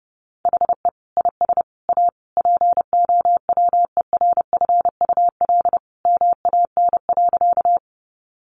Morse code, heard as a keyed tone, about 30 wpm, 700 Hz. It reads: HE IS A POWERFUL MAN.
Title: Sentences from Top 1300 Words 30wpm